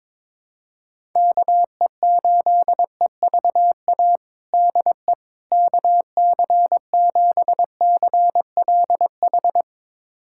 Morse code, heard as a keyed tone, about 22 wpm, 700 Hz. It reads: KE8EVA DE KC7CLH
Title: USA Callsign DE Callsign 22wpm